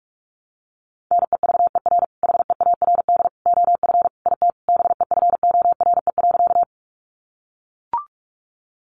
Morse code, heard as a keyed tone, about 44 wpm, 700 Hz. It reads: NEVER HEARD OF IT BEFORE.
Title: Sentences from Top 400 Words 45wpm